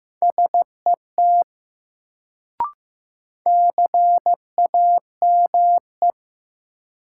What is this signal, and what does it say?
Morse code, heard as a keyed tone, about 15 wpm, 700 Hz. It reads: SET CAME
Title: Sentences from Top 200 Words - New Word Drill - 15wpm